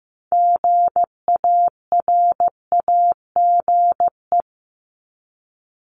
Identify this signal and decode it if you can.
Morse code, heard as a keyed tone, about 15 wpm, 700 Hz. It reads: GARAGE
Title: Multi-letter Phonemes 15wpm